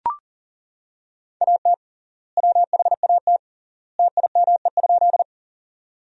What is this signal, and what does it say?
Morse code, heard as a keyed tone, about 40 wpm, 700 Hz. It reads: AT WHAT TIME?
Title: 3x Word-Spacing - Sentences from Top 100 Words - 40wpm